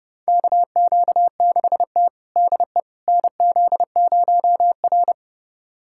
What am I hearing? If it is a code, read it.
Morse code, heard as a keyed tone, about 30 wpm, 700 Hz. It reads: KQ6T DE NZ0R
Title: USA Callsign DE Callsign - 4 Characters - 30wpm